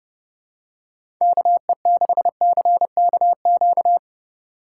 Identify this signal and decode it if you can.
Morse code, heard as a keyed tone, about 31 wpm, 700 Hz. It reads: KE6CKQ
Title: USA Call Signs - 6 Characters - 2x Character Spacing - 30wpm